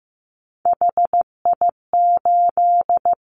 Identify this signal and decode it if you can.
Morse code, heard as a keyed tone, about 15 wpm, 700 Hz. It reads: HI8
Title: Callsign Prefixes for Top 100 DXCC - 15wpm